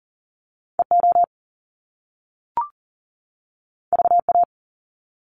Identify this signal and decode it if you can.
Morse code, heard as a keyed tone, about 40 wpm, 700 Hz. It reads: EO VA
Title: Top Two Letter Combinations in Words 40wpm